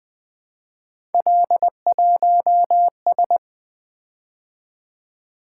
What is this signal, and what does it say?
Morse code, heard as a keyed tone, about 20 wpm, 700 Hz. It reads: L1S
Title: Three Letter-Number Combinations - 20wpm